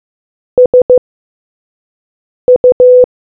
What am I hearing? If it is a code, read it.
Morse code, heard as a keyed tone, about 15 wpm, spaced as for 5 wpm, 500 Hz. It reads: SU